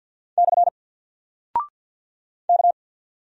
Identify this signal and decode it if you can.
Morse code, heard as a keyed tone, about 48 wpm, 700 Hz. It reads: / K
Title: Single Character (Letter:Number:Punctuation:Prosign) 50wpm